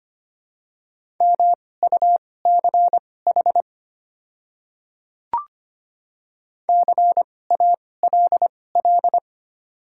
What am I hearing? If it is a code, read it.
Morse code, heard as a keyed tone, about 25 wpm, spaced as for 18 wpm, 700 Hz. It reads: MUCH CALL
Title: 2x Character Spacing - Top 200 Words - 25wpm